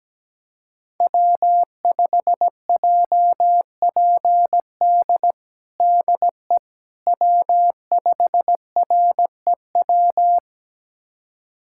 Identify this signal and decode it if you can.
Morse code, heard as a keyed tone, about 17 wpm, 700 Hz. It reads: W5JPD DE W5REW
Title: USA Callsign DE Callsign - 5 Characters - 17wpm